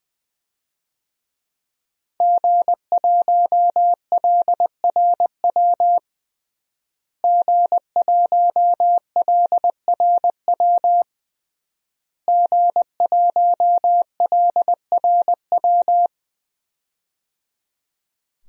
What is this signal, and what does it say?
Morse code, heard as a keyed tone, about 20 wpm, 700 Hz. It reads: G1LRW G1LRW G1LRW
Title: DXCC-100 Call Signs - 5 Characters - Repeated 3x - 20wpm